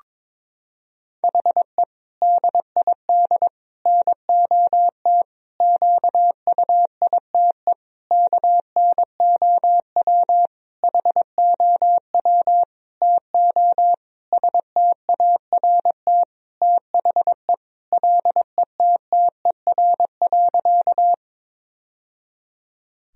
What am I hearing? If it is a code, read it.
Morse code, heard as a keyed tone, about 22 wpm, 700 Hz. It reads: HE DID NOT QUITE KNOW HOW TO START THE LETTER.
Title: Sentences from Top 1000 Words - Review Part 1 - 22wpm